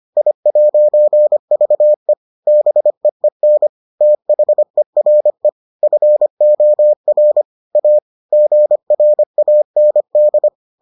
Morse code, heard as a keyed tone, about 25 wpm, 600 Hz. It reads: I'VE BEEN THERE FOR A GRAND